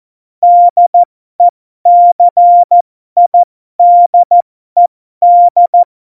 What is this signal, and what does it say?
Morse code, heard as a keyed tone, about 14 wpm, spaced as for 12 wpm, 700 Hz. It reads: DECIDED
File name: Morsekurs / Witze